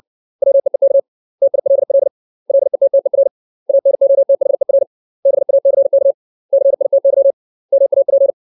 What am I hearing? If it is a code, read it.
Morse code, heard as a keyed tone, about 60 wpm, 550 Hz. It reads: WEEK TILL LETTER ANOTHER BACK CITY MAP
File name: 632 words / 632 Words 60WPM